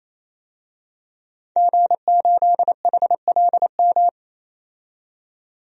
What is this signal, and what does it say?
Morse code, heard as a keyed tone, about 28 wpm, 700 Hz. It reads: G8HLM
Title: DXCC-100 Call Signs - 5 Characters - 28wpm